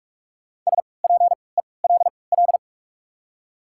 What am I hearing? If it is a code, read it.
Morse code, heard as a keyed tone, about 43 wpm, spaced as for 25 wpm, 690 Hz. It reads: SPELL
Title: 3x Character Spacing - Top 200 Words - 45wpm